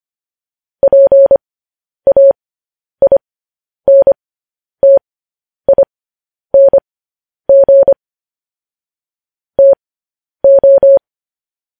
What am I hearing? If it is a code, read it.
Morse code, heard as a keyed tone, about 25 wpm, spaced as for 10 wpm, 550 Hz. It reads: PAINTING TO